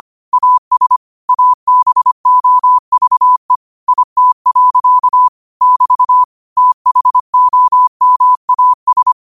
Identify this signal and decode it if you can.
Morse code, heard as a keyed tone, about 25 wpm, 1000 Hz. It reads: AS ABOVE IT. <BT> THOMAS
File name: qotdfiles / 25wpm